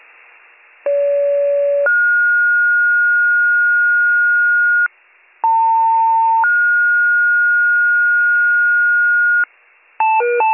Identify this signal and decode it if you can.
Two-tone page, heard as a one-second tone, then a three-second tone, about 570 then 1430 Hz, about 900 then 1430 Hz. It is heard from 0.9 to 9.4 s.